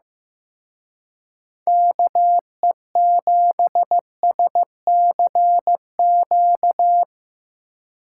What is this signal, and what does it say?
Morse code, heard as a keyed tone, about 17 wpm, 700 Hz. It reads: KE7SCQ
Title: Speed Racing - USA 2x3 Call Signs 15wpm